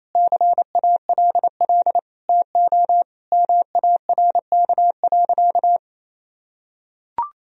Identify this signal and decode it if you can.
Morse code, heard as a keyed tone, about 28 wpm, 700 Hz. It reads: CALL TO MARK.
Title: Sentences from Top 1000 Words - Review Part 2 - 28wpm